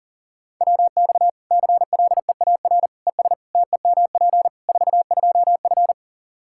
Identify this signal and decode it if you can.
Morse code, heard as a keyed tone, about 37 wpm, 700 Hz. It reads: WX CLEAR ES TEMP 42F
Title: Practice QSO - Exchange 2 - Info - 40wpm